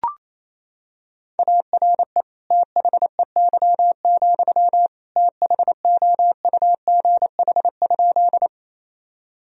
Morse code, heard as a keyed tone, about 28 wpm, 700 Hz. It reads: ARE THEY, THOUGH?
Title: Sentences from Top 1000 Words - Review Part 2 - 28wpm